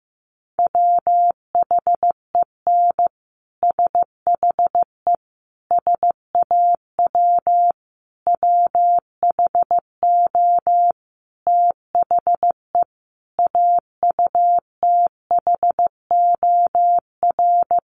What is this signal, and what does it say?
Morse code, heard as a keyed tone, about 15 wpm, 700 Hz. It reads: WHEN SHE SAW WHO THE AUTHOR
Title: Sentences from Top 1200 Words 15wpm